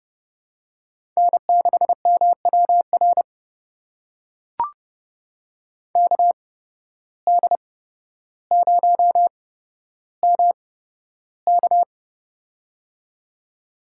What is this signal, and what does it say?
Morse code, heard as a keyed tone, about 30 wpm, spaced as for 13 wpm, 700 Hz. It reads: N6MWR KD0MK
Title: USA Call Signs - 5 Characters - 8x Character Spacing - 30wpm